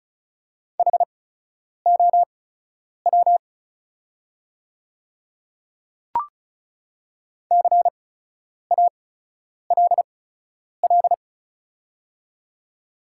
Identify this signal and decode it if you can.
Morse code, heard as a keyed tone, about 36 wpm, spaced as for 10 wpm, 690 Hz. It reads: HOW CALL
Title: 8x Character Spacing - Top 200 Words - 35wpm